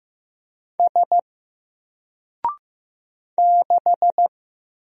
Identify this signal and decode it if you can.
Morse code, heard as a keyed tone, about 15 wpm, 700 Hz. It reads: S 6
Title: Single Letter-Number 15wpm